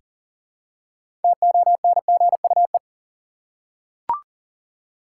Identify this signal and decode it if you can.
Morse code, heard as a keyed tone, about 40 wpm, 700 Hz. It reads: TONGUE
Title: Six Letter Words - 40wpm